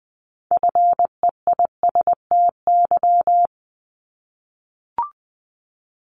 Morse code, heard as a keyed tone, about 20 wpm, 700 Hz. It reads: FEISTY
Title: Multi-letter Phonemes 20wpm